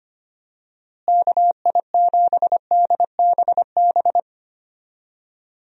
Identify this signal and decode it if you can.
Morse code, heard as a keyed tone, about 25 wpm, 700 Hz. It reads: KI7DBB
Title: USA Call Signs - 6 Characters - Repeated 2x - 25wpm